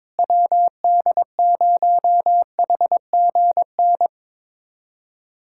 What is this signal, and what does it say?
Morse code, heard as a keyed tone, about 22 wpm, 700 Hz. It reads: WD0HGN
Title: USA Callsign DE Callsign - 6 Characters - 22wpm